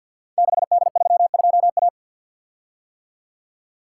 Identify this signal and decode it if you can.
Morse code, heard as a keyed tone, about 46 wpm, 700 Hz. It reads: 6D32A
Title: Code Groups - 5 Letter:Numbers - 50wpm